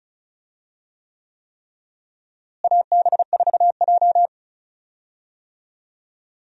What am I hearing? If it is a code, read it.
Morse code, heard as a keyed tone, about 36 wpm, 700 Hz. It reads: AB4J
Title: USA Call Signs - 4 Characters - Repeated 3x - 35wpm